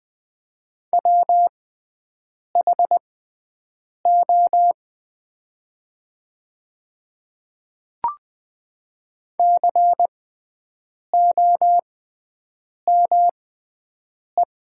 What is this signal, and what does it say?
Morse code, heard as a keyed tone, about 20 wpm, spaced as for 7 wpm, 700 Hz. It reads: WHO COME
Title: 6x Character Spacing - Top 200 Words - 20wpm